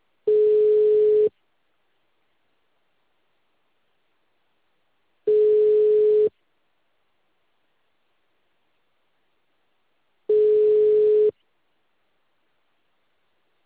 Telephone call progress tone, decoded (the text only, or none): ringback tone